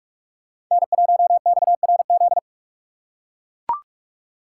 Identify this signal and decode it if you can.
Morse code, heard as a keyed tone, about 41 wpm, 700 Hz. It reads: N1XRZ